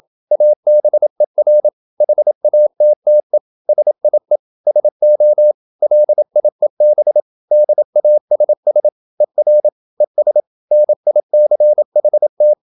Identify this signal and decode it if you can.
Morse code, heard as a keyed tone, about 27 wpm, 600 Hz. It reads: ABER HATTE SIE SO LIEB DASS ER ES NICHT